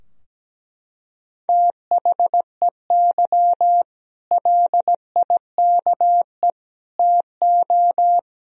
Morse code, heard as a keyed tone, about 17 wpm, 700 Hz. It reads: THEY LIKE TO
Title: Sentences from Top 1000 Words - Review Part 3 - 17wpm